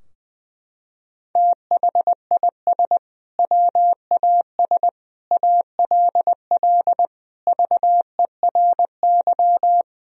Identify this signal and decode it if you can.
Morse code, heard as a keyed tone, about 20 wpm, 700 Hz. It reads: THIS WAS ALL VERY
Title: Sentences from Top 1000 Words - Review Part 3 - 20wpm